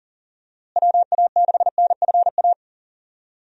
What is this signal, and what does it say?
Morse code, heard as a keyed tone, about 40 wpm, 700 Hz. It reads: WA6NFA
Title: USA Call Signs - 6 Characters - Repeated 2x - 40wpm